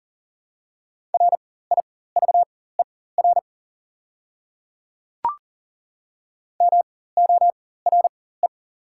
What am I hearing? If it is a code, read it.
Morse code, heard as a keyed tone, about 40 wpm, spaced as for 19 wpm, 700 Hz. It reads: RIVER MORE
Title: 4x Character Spacing - Top 300 Words - 40wpm